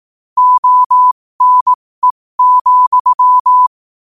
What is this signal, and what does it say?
Morse code, heard as a keyed tone, about 18 wpm, spaced as for 15 wpm, 1000 Hz. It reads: ONE,
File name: qotdfiles / 15wpm